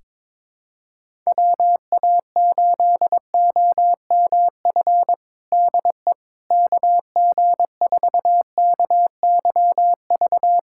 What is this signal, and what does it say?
Morse code, heard as a keyed tone, about 22 wpm, 700 Hz. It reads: WA8OMF DE KG4KYV
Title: USA Callsign DE Callsign - 6 Characters - 22wpm